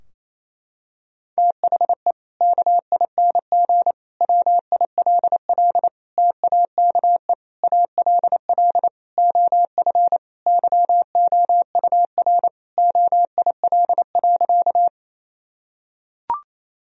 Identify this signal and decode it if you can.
Morse code, heard as a keyed tone, about 28 wpm, 700 Hz. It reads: THE KING WILL TAKE ALL OF YOUR OIL.
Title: Sentences from Top 1000 Words - Review Part 4 - 28wpm